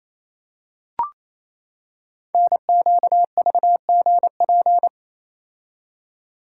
Morse code, heard as a keyed tone, about 29 wpm, 700 Hz. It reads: NQVGP
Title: Code Groups - 5 Letters - 28wpm